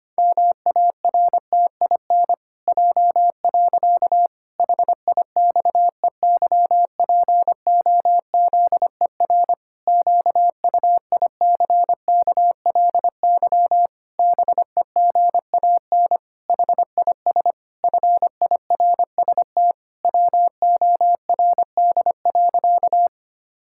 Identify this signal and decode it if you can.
Morse code, heard as a keyed tone, about 25 wpm, 700 Hz. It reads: MARTIN J. HIXEYPOZER QUICKLY BEGAN HIS FIRST WORD.
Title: Pangram Challenge v2 - 25wpm